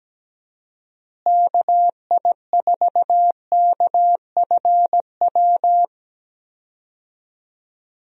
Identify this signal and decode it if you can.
Morse code, heard as a keyed tone, about 17 wpm, 700 Hz. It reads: KI4KFW
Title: USA Call Signs - 6 Characters - Repeated 3x - 17wpm